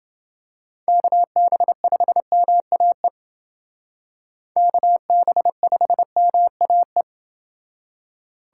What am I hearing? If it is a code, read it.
Morse code, heard as a keyed tone, about 29 wpm, 700 Hz. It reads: KB5MAE KB5MAE